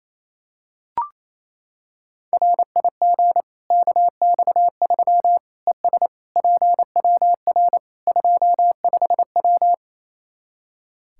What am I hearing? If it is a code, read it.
Morse code, heard as a keyed tone, about 28 wpm, 700 Hz. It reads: RIG KX3 ES PWR 25W
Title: Practice QSO - Exchange 2 - Info - 28wpm